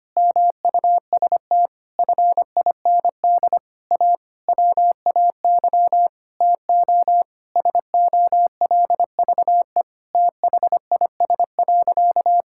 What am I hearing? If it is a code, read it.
Morse code, heard as a keyed tone, about 25 wpm, 700 Hz. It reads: MUST FIND A WAY TO SOLVE THIS.
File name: practice / 25wpm / Sentences with 4-Word Phrases - 25wpm